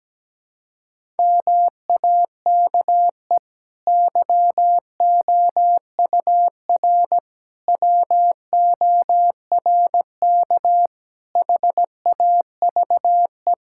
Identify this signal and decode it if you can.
Morse code, heard as a keyed tone, about 17 wpm, 700 Hz. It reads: MAKE YOUR WORK HAVE
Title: Sentences from Top 1000 Words - Review Part 6 - 17wpm